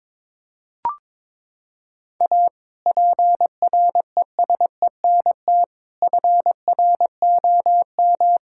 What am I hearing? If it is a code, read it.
Morse code, heard as a keyed tone, about 22 wpm, 700 Hz. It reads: A PRESENT FROM
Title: Sentences from Top 1000 Words - Review Part 5 - 22wpm